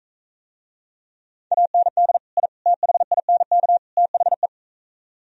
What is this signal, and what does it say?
Morse code, heard as a keyed tone, about 42 wpm, 700 Hz. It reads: AND I THINK THE